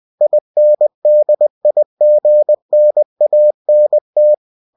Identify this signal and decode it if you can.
Morse code, heard as a keyed tone, about 20 wpm, 600 Hz. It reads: INDIGNANT
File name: cw-books / winnie-the-pooh / winnie-20wpm-page-0071